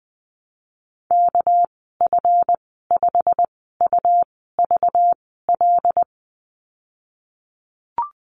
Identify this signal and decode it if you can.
Morse code, heard as a keyed tone, about 20 wpm, spaced as for 15 wpm, 700 Hz. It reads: KF5UVL
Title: 238 Lesson 38g - Letter q - Call Signs 20wpm